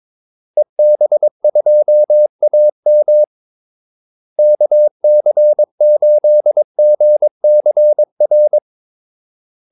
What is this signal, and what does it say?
Morse code, heard as a keyed tone, about 22 wpm, 600 Hz. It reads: EB2AM KC8GCR
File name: Callsigns-22wpm